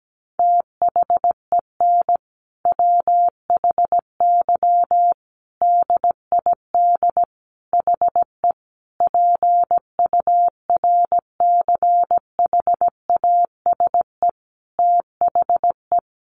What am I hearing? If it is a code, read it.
Morse code, heard as a keyed tone, about 17 wpm, 700 Hz. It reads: THEN WHY DID HE PURCHASE THE
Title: Sentences from Top 800 Words 17wpm